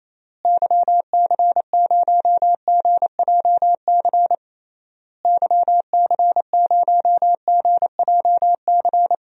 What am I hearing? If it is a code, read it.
Morse code, heard as a keyed tone, about 27 wpm, 700 Hz. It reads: YC0GJC YC0GJC
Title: DXCC-100 Call Signs - 6 Characters - Repeated 3x - 28wpm